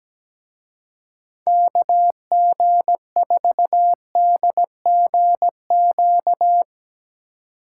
Morse code, heard as a keyed tone, about 17 wpm, 700 Hz. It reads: KG4DGQ
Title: Speed Racing - USA Call Signs 17wpm